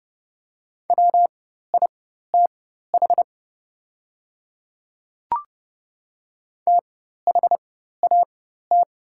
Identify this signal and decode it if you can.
Morse code, heard as a keyed tone, about 30 wpm, spaced as for 14 wpm, 700 Hz. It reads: WITH THAT
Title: 4x Character Spacing - Top 100 Words - 30wpm